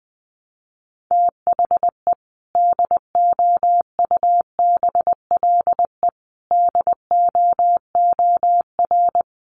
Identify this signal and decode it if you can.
Morse code, heard as a keyed tone, about 20 wpm, 700 Hz. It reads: THE DOUBLE DOOR
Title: Sentences from Top 1400 Words 20wpm